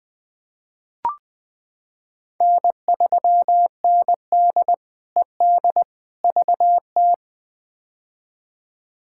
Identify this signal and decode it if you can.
Morse code, heard as a keyed tone, about 20 wpm, 700 Hz. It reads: N3ND ED VT
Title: K1USN SST QSO Practice - 20wpm